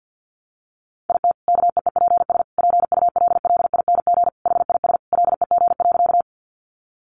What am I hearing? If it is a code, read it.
Morse code, heard as a keyed tone, about 50 wpm, 700 Hz. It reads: IT KEEPS PULLING HIS LEG.
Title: Additional Easy Phrases 50wpm